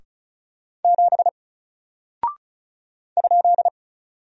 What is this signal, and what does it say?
Morse code, heard as a keyed tone, about 36 wpm, 700 Hz. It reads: 7 ?
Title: Single Character (Letter:Number:Punctuation:Prosign) 35wpm